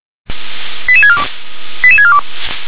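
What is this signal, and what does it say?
Five-tone selective call (ZVEI), heard as quick runs of five tones, tones of about 70 ms. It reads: 88642 88642
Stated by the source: ZVEI five-tone selective call received on an SDR.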